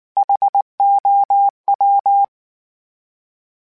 Morse code, heard as a keyed tone, about 19 wpm, 800 Hz. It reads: HOW
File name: H Prefixes / H mp3s